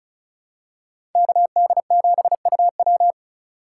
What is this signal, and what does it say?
Morse code, heard as a keyed tone, about 36 wpm, 700 Hz. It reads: KD7UW
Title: USA Call Signs - 8x Character Spacing - 35wpm